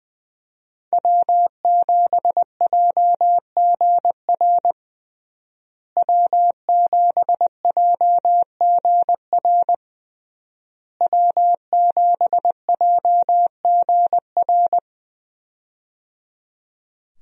Morse code, heard as a keyed tone, about 20 wpm, 700 Hz. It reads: W7JGR W7JGR W7JGR
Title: USA Call Signs - 5 Characters - Repeated 3x - 20wpm